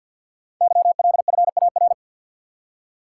Morse code, heard as a keyed tone, about 51 wpm, 700 Hz. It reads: YLFAR
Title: Code Groups - 5 Letters - 50wpm